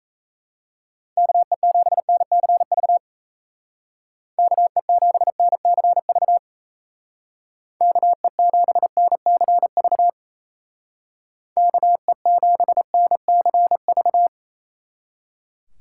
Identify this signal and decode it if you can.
Morse code, heard as a keyed tone, about 36 wpm, 700 Hz. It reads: KE7NCV KE7NCV KE7NCV KE7NCV